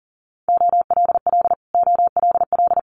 Morse code, heard as a keyed tone, about 38 wpm, 700 Hz. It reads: OLL OLL
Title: Multi-letter Phonemes Repeated 3x 40wpm